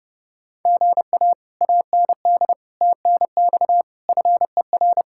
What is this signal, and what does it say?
Morse code, heard as a keyed tone, about 39 wpm, 700 Hz. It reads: GA AND TNX FER